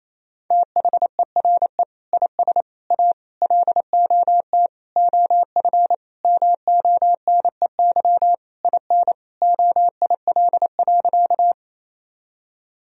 Morse code, heard as a keyed tone, about 28 wpm, 700 Hz. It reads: THERE IS A LOT OF MONEY IN OIL.
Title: Sentences from Top 1000 Words - Review Part 4 - 28wpm